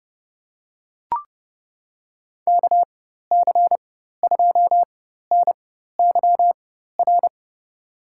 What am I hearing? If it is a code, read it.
Morse code, heard as a keyed tone, about 30 wpm, spaced as for 14 wpm, 700 Hz. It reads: KC2NYR
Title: USA Call Signs - 6 Characters - 4x Character Spacing - 30wpm